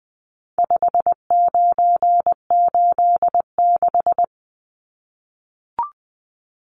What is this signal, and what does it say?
Morse code, heard as a keyed tone, about 20 wpm, 700 Hz. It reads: 5986
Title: Numbers - 4-Digit 20wpm